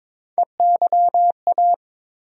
Morse code, heard as a keyed tone, about 22 wpm, 700 Hz. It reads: EYA
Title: USA Call Sign Postfixes - 3 Letters 22wpm